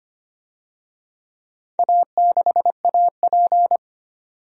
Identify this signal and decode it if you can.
Morse code, heard as a keyed tone, about 25 wpm, 700 Hz. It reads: A6AP
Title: DXCC-100 Call Signs - 4 Characters - 25wpm